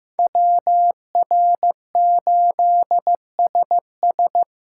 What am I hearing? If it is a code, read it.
Morse code, heard as a keyed tone, about 18 wpm, 700 Hz. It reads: WR8SS
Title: Speed Racing - USA 2x2 Call Signs 15wpm